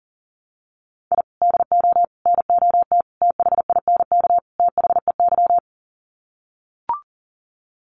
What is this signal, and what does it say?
Morse code, heard as a keyed tone, about 40 wpm, 700 Hz. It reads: I DO NOT THINK THEY
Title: Sets of 5 Words 40wpm